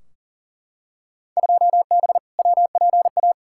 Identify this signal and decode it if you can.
Morse code, heard as a keyed tone, about 39 wpm, 700 Hz. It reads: 2B WPA
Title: Field Day Exchanges Encore - 40wpm